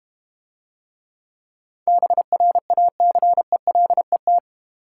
Morse code, heard as a keyed tone, about 31 wpm, 700 Hz. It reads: BRACELET
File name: Easy Words - Speed-Racing with Hints - 25wpm